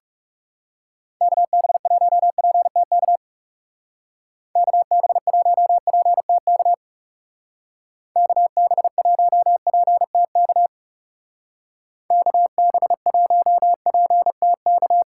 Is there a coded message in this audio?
Yes: Morse code from 1.2 to 15.0 s, heard as a keyed tone, about 38 wpm, 700 Hz. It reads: KB1PTK KB1PTK KB1PTK KB1PTK